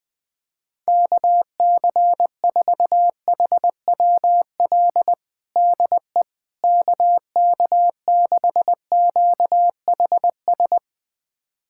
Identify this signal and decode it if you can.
Morse code, heard as a keyed tone, about 21 wpm, 700 Hz. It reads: KC4HWL DE KK6QHS
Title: USA Callsign DE Callsign 20wpm